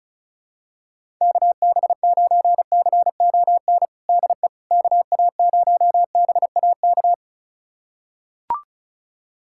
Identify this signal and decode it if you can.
Morse code, heard as a keyed tone, about 36 wpm, 700 Hz. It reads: KB9CON DE KA0BAK
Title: USA Callsign DE Callsign 35wpm